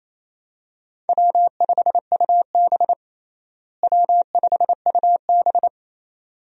Morse code, heard as a keyed tone, about 28 wpm, 700 Hz. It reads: W5UB W5UB
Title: USA Call Signs - 4 Characters - Repeated 2x - 28wpm